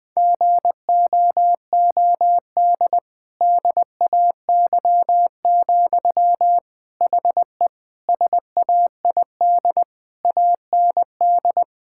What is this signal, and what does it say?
Morse code, heard as a keyed tone, about 20 wpm, 700 Hz. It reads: GOOD DAY, HE SAID AND
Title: Sentences from Top 1600 Words 20wpm